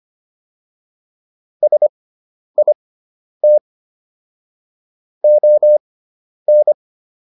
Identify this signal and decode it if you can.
Morse code, heard as a keyed tone, about 25 wpm, spaced as for 10 wpm, 600 Hz. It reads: SIT ON